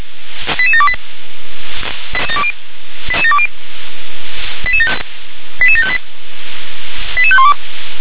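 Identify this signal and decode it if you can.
Five-tone selective call (ZVEI), heard as quick runs of five tones, tones of about 70 ms. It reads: 88728 88720 88758 88512
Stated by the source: ZVEI five-tone call received on SDR.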